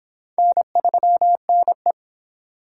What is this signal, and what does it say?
Morse code, heard as a keyed tone, about 28 wpm, 700 Hz. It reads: N3NE
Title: Speed Racing - USA Call Signs 22wpm